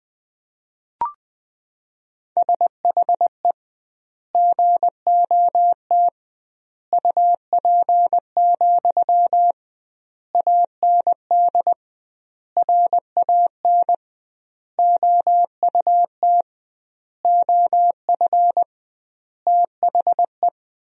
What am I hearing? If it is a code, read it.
Morse code, heard as a keyed tone, about 20 wpm, 700 Hz. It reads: SHE GOT UP, AND RAN OUT OF THE
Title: 2x Word-Spacing - Sentences from Top 500 Words - 20wpm